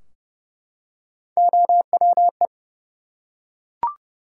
Morse code, heard as a keyed tone, about 30 wpm, 700 Hz. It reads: OWE
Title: Three Letter Words - 30wpm